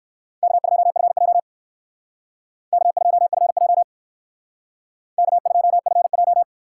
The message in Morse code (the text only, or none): K2FJ K2FJ K2FJ